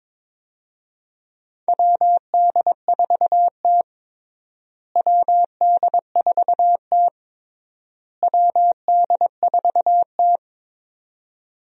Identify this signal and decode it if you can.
Morse code, heard as a keyed tone, about 22 wpm, 700 Hz. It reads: WD4T WD4T WD4T
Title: USA Call Signs - 4 Characters - Repeated 3x - 22wpm